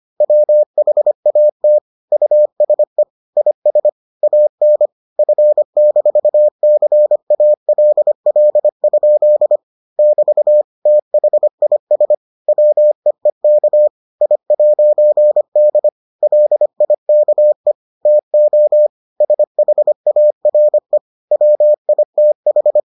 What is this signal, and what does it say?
Morse code, heard as a keyed tone, about 25 wpm, 600 Hz. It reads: WHAT USE IS AN F-CALL? <BT> THIS WEEK I'D LIKE TO SHARE WITH